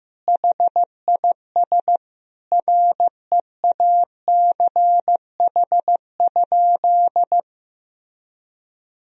Morse code, heard as a keyed tone, about 15 wpm, 700 Hz. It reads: HIS REACH?
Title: Sentences from Top 1000 Words - Review Part 4 - 15wpm